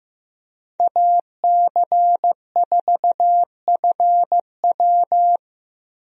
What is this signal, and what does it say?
Morse code, heard as a keyed tone, about 15 wpm, 700 Hz. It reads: AC4FW
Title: USA Call Signs - 5 Characters - Repeated 3x - 15wpm